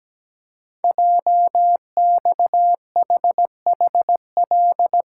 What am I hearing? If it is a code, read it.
Morse code, heard as a keyed tone, about 17 wpm, 700 Hz. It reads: JXHHL
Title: Code Groups - 5 Letters - 17wpm